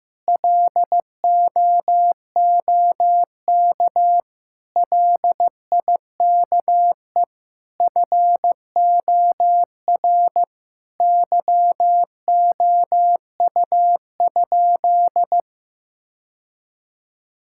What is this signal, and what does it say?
Morse code, heard as a keyed tone, about 15 wpm, 700 Hz. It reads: LOOK LIKE FOR YOU?
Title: Sentences from Top 1000 Words - Review Part 6 - 15wpm